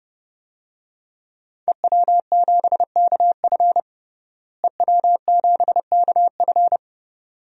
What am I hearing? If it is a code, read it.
Morse code, heard as a keyed tone, about 30 wpm, 700 Hz. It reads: EW7KF EW7KF
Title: DXCC-100 Call Signs - 5 Characters - Repeated 2x - 30wpm